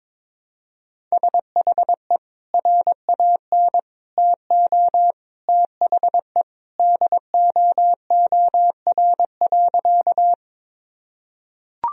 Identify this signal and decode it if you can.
Morse code, heard as a keyed tone, about 22 wpm, 700 Hz. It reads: SHE RAN TO THE DOOR.
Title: Sentences from Top 1000 Words - Review Part 3 - 22wpm